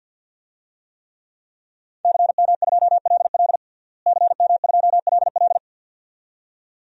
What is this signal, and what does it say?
Morse code, heard as a keyed tone, about 52 wpm, 700 Hz. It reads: CM2LL CM2LL
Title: DXCC-100 Call Signs - 5 Characters - Repeated 2x - 50wpm